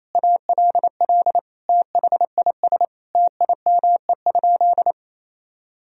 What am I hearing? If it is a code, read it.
Morse code, heard as a keyed tone, about 27 wpm, 690 Hz. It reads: ALL THIS TIME?
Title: Sentences from Top 1000 Words - Review Part 3 - 28wpm